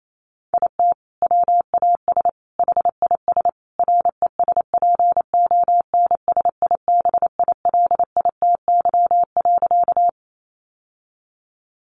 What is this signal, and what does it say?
Morse code, heard as a keyed tone, about 28 wpm, 700 Hz. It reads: IT WAS HIS RESPONSIBILITY.
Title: Sentences from Top 1200 Words 28wpm